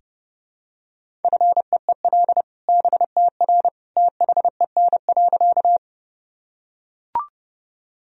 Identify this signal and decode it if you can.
Morse code, heard as a keyed tone, about 30 wpm, 700 Hz. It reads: FEEL BTR THEN.
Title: Sentences from Top 400 Words and Abbreviations 30wpm